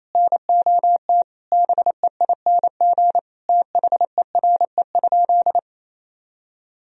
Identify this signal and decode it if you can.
Morse code, heard as a keyed tone, about 27 wpm, 700 Hz. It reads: NOT BEING THERE?
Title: Sentences from Top 1000 Words - Review Part 3 - 28wpm